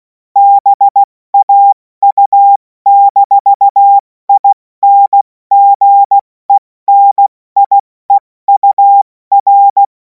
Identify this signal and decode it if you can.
Morse code, heard as a keyed tone, about 16 wpm, 800 Hz. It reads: BAU-INGENIEUR
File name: Morsekurs / Witze / Witz 24. Februar 2026 70 F80